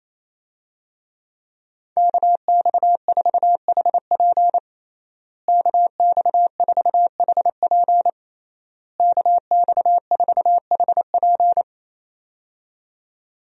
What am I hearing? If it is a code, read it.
Morse code, heard as a keyed tone, about 27 wpm, 700 Hz. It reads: KX4HP KX4HP KX4HP
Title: USA Call Signs - 5 Characters - Repeated 3x - 28wpm